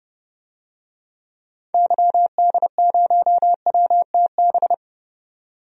Morse code, heard as a keyed tone, about 30 wpm, 700 Hz. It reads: YD0WTB
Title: DXCC-100 Call Signs - 6 Characters - 30wpm